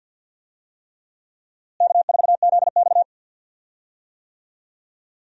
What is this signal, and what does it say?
Morse code, heard as a keyed tone, about 52 wpm, 700 Hz. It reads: K4ZX
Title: USA Call Signs - 4 Characters - Repeated 3x - 50wpm